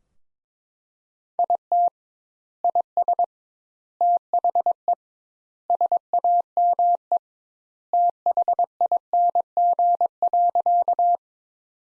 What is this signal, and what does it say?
Morse code, heard as a keyed tone, about 22 wpm, 700 Hz. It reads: IT IS THE SAME THING.
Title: Sentences Top 200 Words - 22wpm